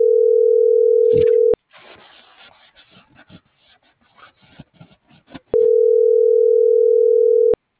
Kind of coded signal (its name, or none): telephone call progress tone